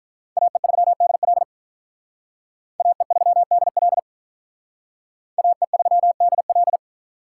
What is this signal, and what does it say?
Morse code, heard as a keyed tone, about 45 wpm, 700 Hz. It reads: AE3DL AE3DL AE3DL